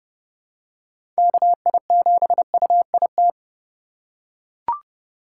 Morse code, heard as a keyed tone, about 30 wpm, 700 Hz. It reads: KI7UIT
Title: USA Call Signs - 6 Characters - 6x Character Spacing - 30wpm